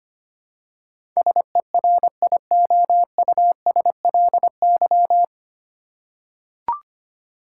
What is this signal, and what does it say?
Morse code, heard as a keyed tone, about 25 wpm, 700 Hz. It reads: SERIOUSLY
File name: Nine Letter Words - 25wpm